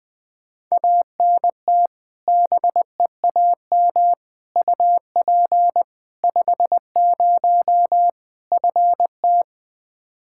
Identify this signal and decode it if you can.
Morse code, heard as a keyed tone, about 20 wpm, 700 Hz. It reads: ANT BEAM UP 50 FT
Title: K1USN SST QSO Practice 20wpm